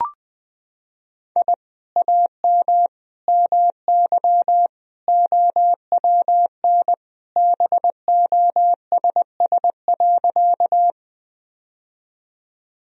Morse code, heard as a keyed tone, about 20 wpm, 700 Hz. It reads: I AM MY OWN BOSS.
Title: Sentences from Top 1000 Words - Review Part 4 - 20wpm